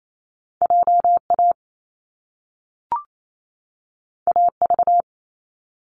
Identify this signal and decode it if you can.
Morse code, heard as a keyed tone, about 28 wpm, 700 Hz. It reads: J A A V
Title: Top Two Letter Combinations in Words 28wpm